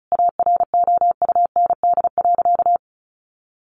Morse code, heard as a keyed tone, about 34 wpm, 700 Hz. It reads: AROUND.
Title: Sentences from Top 800 Words 35wpm